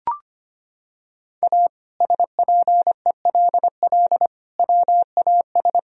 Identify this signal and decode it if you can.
Morse code, heard as a keyed tone, about 25 wpm, 700 Hz. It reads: A SPELL WAS
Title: Sentences from Top 1000 Words - Review Part 1 - 25wpm